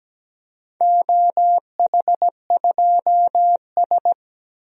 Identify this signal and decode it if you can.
Morse code, heard as a keyed tone, about 17 wpm, 700 Hz. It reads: OH2S
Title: DXCC-100 Call Signs - 4 Characters - Repeated 2x - 17wpm